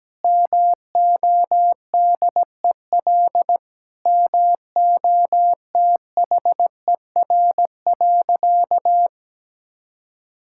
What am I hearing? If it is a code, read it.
Morse code, heard as a keyed tone, about 17 wpm, 700 Hz. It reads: MODEL MOTHER.
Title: Sentences from Top 1000 Words - Review Part 5 - 17wpm